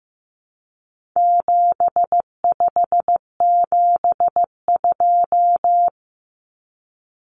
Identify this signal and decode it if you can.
Morse code, heard as a keyed tone, about 15 wpm, 700 Hz. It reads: 7572
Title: Numbers - 4-Digit 15wpm